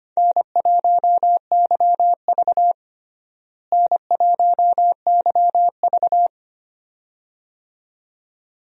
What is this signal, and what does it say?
Morse code, heard as a keyed tone, about 25 wpm, 700 Hz. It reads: N1YV N1YV
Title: USA Call Signs - 4 Characters - Repeated 2x - 25wpm